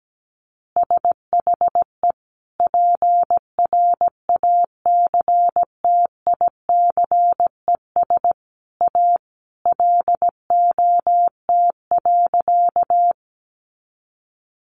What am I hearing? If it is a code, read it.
Morse code, heard as a keyed tone, about 17 wpm, 700 Hz. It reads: SHE PRACTICES A LOT.
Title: Additional Easy Phrases 17wpm